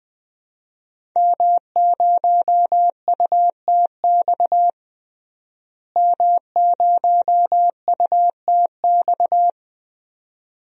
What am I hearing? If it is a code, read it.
Morse code, heard as a keyed tone, about 20 wpm, 700 Hz. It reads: M0UTX M0UTX
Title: DXCC-100 Call Signs - 5 Characters - Repeated 2x - 20wpm